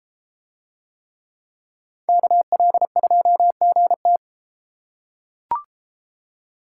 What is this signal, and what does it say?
Morse code, heard as a keyed tone, about 31 wpm, 700 Hz. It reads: KL2GT